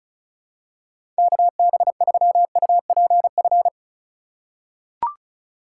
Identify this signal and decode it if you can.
Morse code, heard as a keyed tone, about 36 wpm, 700 Hz. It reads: KB3UPF
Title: USA Call Signs - 3x Character Spacing - 35wpm